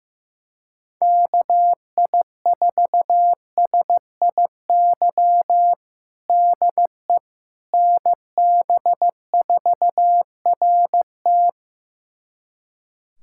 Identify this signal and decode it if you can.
Morse code, heard as a keyed tone, about 15 wpm, 700 Hz. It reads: KI4SIY DE NB4RT
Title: USA Callsign DE Callsign 15wpm